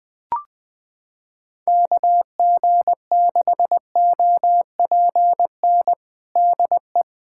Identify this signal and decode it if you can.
Morse code, heard as a keyed tone, about 21 wpm, 700 Hz. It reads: KG6OPN DE
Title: USA Callsign DE Callsign 20wpm